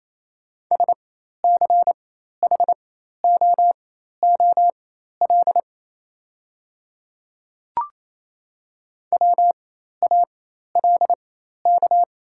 Morse code, heard as a keyed tone, about 28 wpm, spaced as for 13 wpm, 700 Hz. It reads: SCHOOL WALK
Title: 4x Character Spacing - Top 500 Words - 28wpm